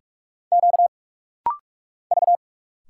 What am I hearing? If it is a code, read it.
Morse code, heard as a keyed tone, about 46 wpm, 700 Hz. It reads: Q V
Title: Single Letters - Rapid-Fire 45wpm